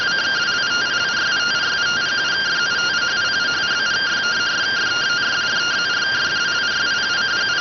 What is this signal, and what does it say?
Two-tone teleprinter signal, about 170 Hz shift.